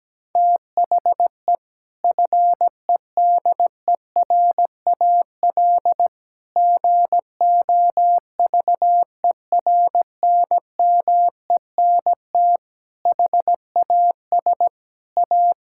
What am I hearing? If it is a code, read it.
Morse code, heard as a keyed tone, about 17 wpm, 700 Hz. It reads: THE FEDERAL GOVERNMENT HAS A
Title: Sentences from Top 1000 Words - Review Part 6 - 17wpm